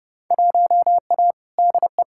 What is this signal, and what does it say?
Morse code, heard as a keyed tone, about 30 wpm, 700 Hz. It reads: 1A DE
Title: Field Day Exchanges Encore - 30wpm